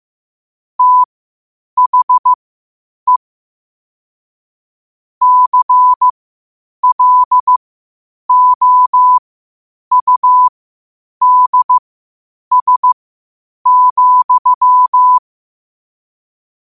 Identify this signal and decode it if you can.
Morse code, heard as a keyed tone, about 15 wpm, spaced as for 8 wpm, 1000 Hz. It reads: THE CLOUDS,
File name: qotdfiles / 8wpm